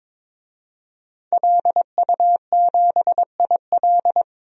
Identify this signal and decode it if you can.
Morse code, heard as a keyed tone, about 22 wpm, 700 Hz. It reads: LU7IL
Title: DXCC-100 Call Signs - 5 Characters - 22wpm